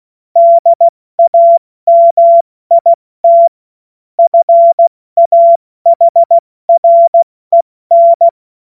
Morse code, heard as a keyed tone, about 16 wpm, spaced as for 14 wpm, 680 Hz. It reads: DAMIT FAHREN
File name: Morsekurs / Witze / Witz 14. Februar 2023 70 F80